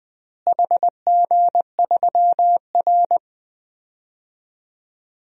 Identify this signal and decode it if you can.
Morse code, heard as a keyed tone, about 20 wpm, 700 Hz. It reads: HG3R
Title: DXCC-100 Call Signs - 4 Characters - Repeated 2x - 20wpm